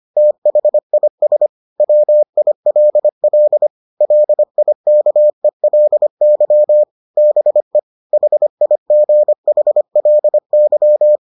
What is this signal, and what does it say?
Morse code, heard as a keyed tone, about 25 wpm, 600 Hz. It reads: THIS WILL LIKELY BE HIGHLY